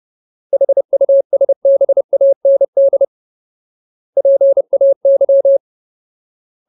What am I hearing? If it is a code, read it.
Morse code, heard as a keyed tone, about 30 wpm, 550 Hz. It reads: HUSBAND PAY
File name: Ultimate Word List / 30WPM Ultimate